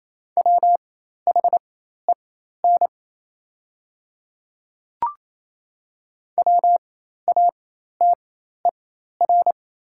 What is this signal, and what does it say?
Morse code, heard as a keyed tone, about 28 wpm, spaced as for 13 wpm, 700 Hz. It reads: WHEN WATER
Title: 4x Character Spacing - Top 100 Words - 28wpm